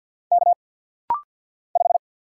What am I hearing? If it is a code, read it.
Morse code, heard as a keyed tone, about 50 wpm, 700 Hz. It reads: K 5
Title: Single Letter-Number Alt - Rapid-Fire 50wpm